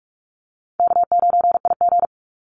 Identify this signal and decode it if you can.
Morse code, heard as a keyed tone, about 44 wpm, 700 Hz. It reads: K9IG
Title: Easy Call Signs Repeated 3x - 45wpm